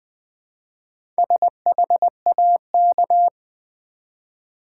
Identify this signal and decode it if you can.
Morse code, heard as a keyed tone, about 20 wpm, 700 Hz. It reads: SHAK
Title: Build A Word - 1 Letter At A Time - 5-Letter Words - 20wpm